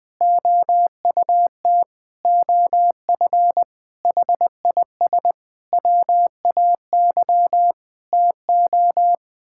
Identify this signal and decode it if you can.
Morse code, heard as a keyed tone, about 20 wpm, 700 Hz. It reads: OUT OF HIS WAY TO
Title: Sets of 6 Words - 20wpm